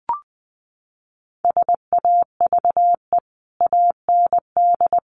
Morse code, heard as a keyed tone, about 20 wpm, 700 Hz. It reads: SAVE AND
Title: Sentences from Top 1100 Words 20wpm